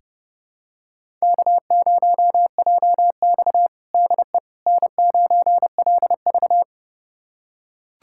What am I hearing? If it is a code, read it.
Morse code, heard as a keyed tone, about 30 wpm, 700 Hz. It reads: K0JX DE N9LV
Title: USA Callsign DE Callsign - 4 Characters - 30wpm